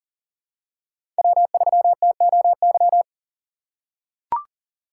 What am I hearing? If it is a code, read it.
Morse code, heard as a keyed tone, about 40 wpm, 700 Hz. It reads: W3TOY
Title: USA Call Signs - 5 Characters - 6x Character Spacing - 40wpm